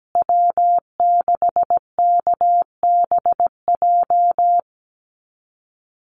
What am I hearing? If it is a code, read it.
Morse code, heard as a keyed tone, about 17 wpm, 700 Hz. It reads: W6KBJ
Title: USA Call Signs Repeated 3x 17wpm